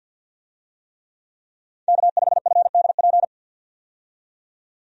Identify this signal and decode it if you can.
Morse code, heard as a keyed tone, about 47 wpm, 700 Hz. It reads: K5FDP
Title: USA Call Signs - 5 Characters - 50wpm